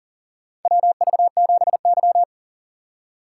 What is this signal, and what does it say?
Morse code, heard as a keyed tone, about 39 wpm, 700 Hz. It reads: WV7Y
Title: USA Call Signs - 3x Character Spacing - 40wpm